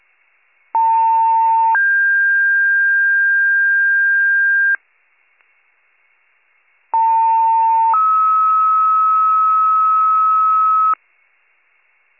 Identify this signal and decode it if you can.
Two-tone page, heard as a one-second tone, then a three-second tone, about 900 then 1600 Hz, about 900 then 1290 Hz. It is heard from 0.7 to 10.9 s.